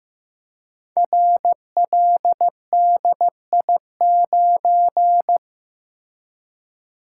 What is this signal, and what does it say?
Morse code, heard as a keyed tone, about 15 wpm, 700 Hz. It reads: RLDI9
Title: Code Groups - 5 Letter:Numbers - 15wpm